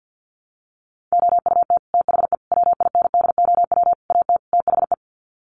Morse code, heard as a keyed tone, about 48 wpm, 700 Hz. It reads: OUT THE WINDOW AT THE
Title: Sets of 5 Words Encore2 50wpm